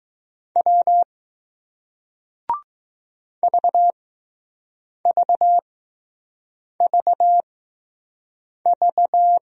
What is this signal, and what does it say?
Morse code, heard as a keyed tone, about 20 wpm, spaced as for 6 wpm, 700 Hz. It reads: W VVVV